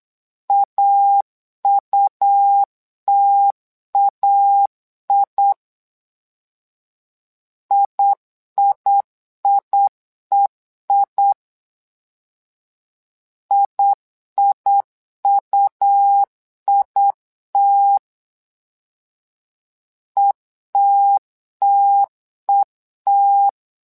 Morse code, heard as a keyed tone, about 8 wpm, 800 Hz. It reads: AUTAI IIIEI IIUIT ETTET
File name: LEZIONI / 2 UHDN / 2 U 8wpm 2028